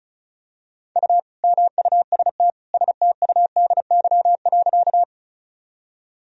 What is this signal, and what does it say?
Morse code, heard as a keyed tone, about 36 wpm, 700 Hz. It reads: U MUST STUDY.
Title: Sentences from Top 300 Words and Abbreviations 35wpm